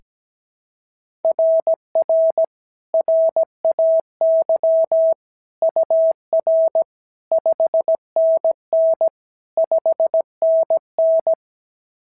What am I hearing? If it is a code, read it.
Morse code, heard as a keyed tone, about 16 wpm, 650 Hz. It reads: RR RAY UR 5NN 5NN
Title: SOTA QSO Practice 17wpm